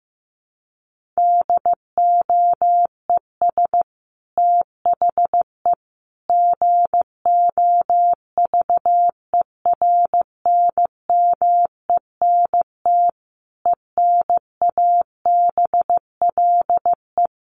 Morse code, heard as a keyed tone, about 15 wpm, 700 Hz. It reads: DOES THE GOVERNMENT ENABLE
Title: Sentences from Top 1500 Words 15wpm